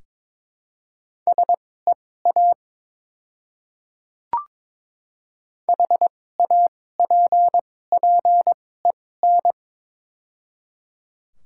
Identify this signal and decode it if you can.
Morse code, heard as a keyed tone, about 22 wpm, spaced as for 16 wpm, 700 Hz. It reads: SEA HAPPEN
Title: 2x Character Spacing - Top 500 Words - 22wpm